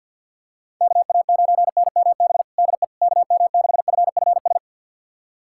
Morse code, heard as a keyed tone, about 48 wpm, 700 Hz. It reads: KA9NMB DE KM6FFS
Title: USA Callsign DE Callsign 50wpm